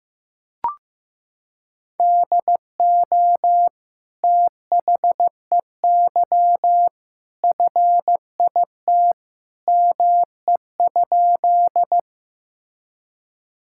Morse code, heard as a keyed tone, about 15 wpm, 700 Hz. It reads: DO THEY FIT ME?
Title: Sentences from Top 1000 Words - Review Part 5 - 15wpm